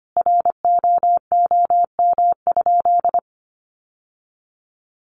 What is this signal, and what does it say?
Morse code, heard as a keyed tone, about 25 wpm, 700 Hz. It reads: ROOM?
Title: Sentences from Top 1200 Words 25wpm